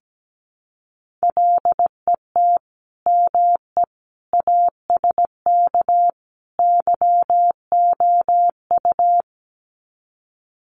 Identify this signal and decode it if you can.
Morse code, heard as a keyed tone, about 17 wpm, 700 Hz. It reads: LET ME ASK YOU
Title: Sets of 4 Words 17wpm